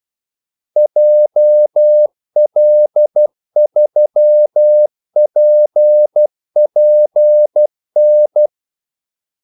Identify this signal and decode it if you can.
Morse code, heard as a keyed tone, about 12 wpm, 600 Hz. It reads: JL3PPN
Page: 2 ekstra ordmellemrum mellem kaldesignalerne